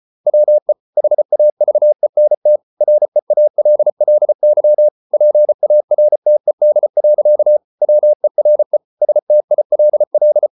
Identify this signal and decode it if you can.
Morse code, heard as a keyed tone, about 34 wpm, 600 Hz. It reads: WE HAVENT REALLY PARTED. WERE STILL